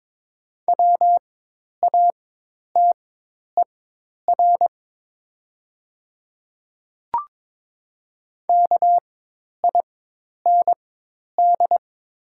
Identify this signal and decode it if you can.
Morse code, heard as a keyed tone, about 22 wpm, spaced as for 10 wpm, 700 Hz. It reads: WATER KIND
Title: 4x Character Spacing - Top 200 Words - 22wpm